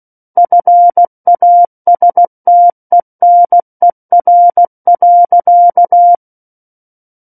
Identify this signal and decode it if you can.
Morse code, heard as a keyed tone, about 16 wpm, 700 Hz. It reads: FASTENER.
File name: Book One / 07-Chapter 7 - 16 WPM